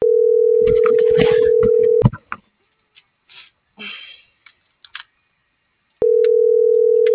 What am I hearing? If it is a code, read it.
Ringback tone (North America), 440 + 480 Hz.